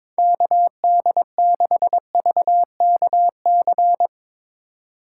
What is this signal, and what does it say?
Morse code, heard as a keyed tone, about 26 wpm, 700 Hz. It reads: KD6VKC
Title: Speed Racing - USA 2x3 Call Signs 22wpm